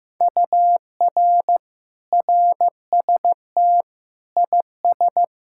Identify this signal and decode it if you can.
Morse code, heard as a keyed tone, about 15 wpm, 700 Hz. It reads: UR RST IS
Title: Speed-Racing Common QSO Phrases Encore - 15wpm